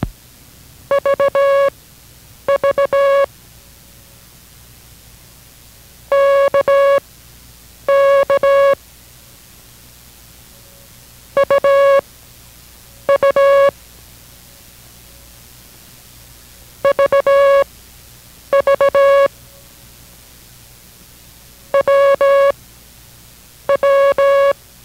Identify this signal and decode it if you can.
Morse code, heard as a keyed tone, about 17 wpm, spaced as for 7 wpm, 560 Hz. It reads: VV KK UU VV WW